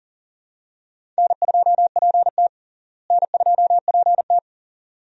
Morse code, heard as a keyed tone, about 38 wpm, 700 Hz. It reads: N2PT N2PT
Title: USA Call Signs - 4 Characters - Repeated 2x - 40wpm